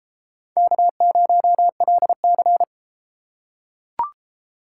Morse code, heard as a keyed tone, about 31 wpm, 700 Hz. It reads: K0LC